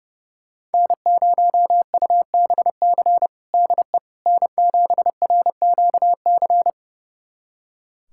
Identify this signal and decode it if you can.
Morse code, heard as a keyed tone, about 30 wpm, 700 Hz. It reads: N0UBC DE N7RQC
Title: USA Callsign DE Callsign - 5 Characters - 30wpm